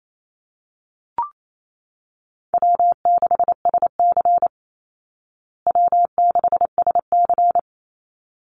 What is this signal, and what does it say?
Morse code, heard as a keyed tone, about 28 wpm, 700 Hz. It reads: W6SC W6SC
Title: Call Signs Repeated 3x - 28wpm